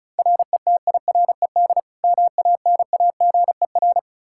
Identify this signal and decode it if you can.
Morse code, heard as a keyed tone, about 34 wpm, 700 Hz. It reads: RETIRED MANAGER
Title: K1USN SST QSO Practice 35wpm